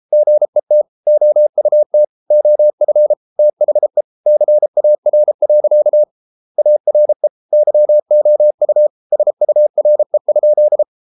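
Morse code, heard as a keyed tone, about 33 wpm, 600 Hz. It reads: GET OUT OF THE CAR. ARE YOU SURE?
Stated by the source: Morse code practice at 32 wpm from CW med Gnister, reading Ken Follett's novel Never in English.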